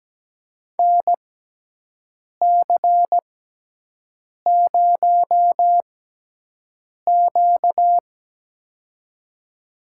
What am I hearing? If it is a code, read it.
Morse code, heard as a keyed tone, about 17 wpm, spaced as for 6 wpm, 700 Hz. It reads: NC0Q
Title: USA Call Signs - 4 Characters - 6x Character Spacing - 17wpm